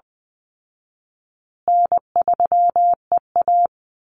Morse code, heard as a keyed tone, about 20 wpm, 700 Hz. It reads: N3EA
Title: Easy Call Signs Repeated 3x - 20wpm